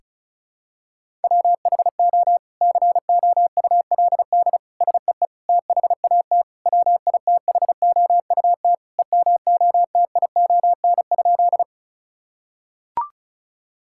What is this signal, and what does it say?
Morse code, heard as a keyed tone, about 34 wpm, 700 Hz. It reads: WHO COULD SEE THAT WITHOUT EMOTION?
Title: Sentences from Top 1600 Words 35wpm